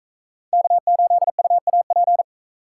Morse code, heard as a keyed tone, about 41 wpm, 700 Hz. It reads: K8UAP